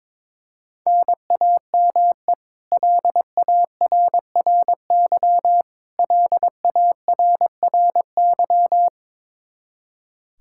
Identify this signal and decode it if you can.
Morse code, heard as a keyed tone, about 22 wpm, 700 Hz. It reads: NAME LARRY LARRY
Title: Practice QSO - Exchange 1 - Beginning - 22wpm